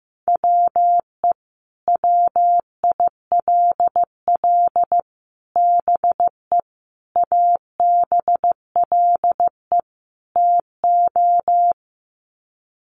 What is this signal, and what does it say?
Morse code, heard as a keyed tone, about 15 wpm, 700 Hz. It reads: WE WILL BE ABLE TO
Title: Sets of 5 Words Encore 15wpm